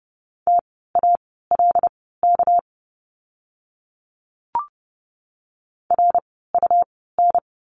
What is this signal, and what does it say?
Morse code, heard as a keyed tone, about 30 wpm, spaced as for 17 wpm, 700 Hz. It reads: TALK RUN
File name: course / 30:10wpm / 237 Lesson 38f - Letter q - Words 30wpm